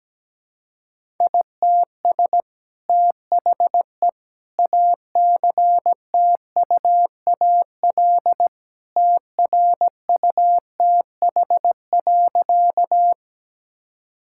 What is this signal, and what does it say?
Morse code, heard as a keyed tone, about 17 wpm, 700 Hz. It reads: ITS THE ACTUAL TRUTH.
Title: Sentences from Top 1000 Words - Review Part 5 - 17wpm